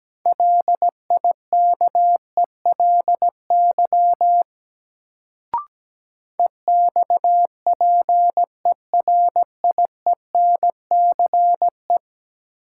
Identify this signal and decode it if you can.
Morse code, heard as a keyed tone, about 17 wpm, 700 Hz. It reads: LIKELY EXPERIENCE
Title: Sentences from Top 600 Words - New Word Drill - 17wpm